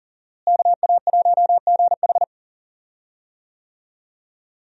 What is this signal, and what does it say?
Morse code, heard as a keyed tone, about 40 wpm, 700 Hz. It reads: KA1GH
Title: USA Call Signs - 5 Characters - Repeated 3x - 40wpm